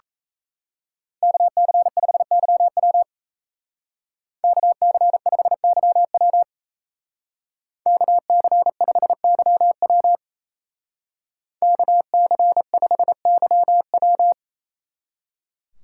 Morse code, heard as a keyed tone, about 34 wpm, 700 Hz. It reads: KC5YW KC5YW KC5YW KC5YW
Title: Speed Racing - USA Call Signs 28wpm